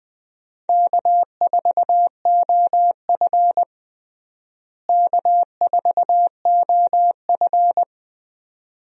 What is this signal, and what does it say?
Morse code, heard as a keyed tone, about 20 wpm, 700 Hz. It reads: K4OF K4OF
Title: USA Call Signs - 4 Characters - Repeated 2x - 20wpm